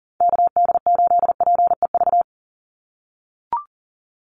Morse code, heard as a keyed tone, about 40 wpm, 700 Hz. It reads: KD8PEV
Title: USA Call Signs 40wpm